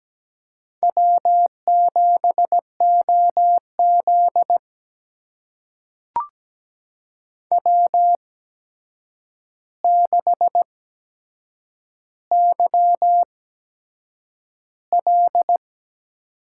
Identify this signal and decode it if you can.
Morse code, heard as a keyed tone, about 17 wpm, spaced as for 7 wpm, 700 Hz. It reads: W7OZ W6YL
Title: USA Call Signs - 4 Characters - 8x Character Spacing - 17wpm